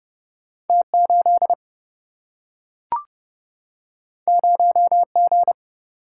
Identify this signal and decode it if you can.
Morse code, heard as a keyed tone, about 30 wpm, 700 Hz. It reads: T8 0G
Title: Single-Letter Single-Number 30wpm